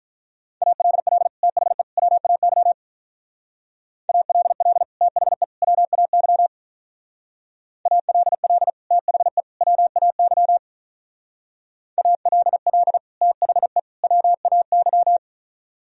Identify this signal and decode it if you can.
Morse code, heard as a keyed tone, about 43 wpm, 700 Hz. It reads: ALL THE WAY ALL THE WAY ALL THE WAY ALL THE WAY
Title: Speed-Racing - Sets of 3 Words 35wpm